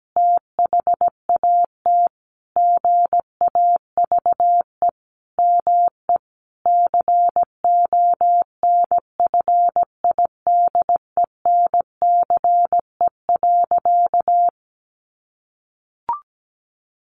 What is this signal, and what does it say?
Morse code, heard as a keyed tone, about 17 wpm, 700 Hz. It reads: THAT GAVE ME CONFIDENCE.
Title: Sentences from Top 1500 Words 17wpm